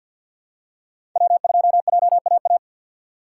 Morse code, heard as a keyed tone, about 53 wpm, 700 Hz. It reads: W 2 J A A
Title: USA Call Signs - 5 Characters - 6x Character Spacing - 50wpm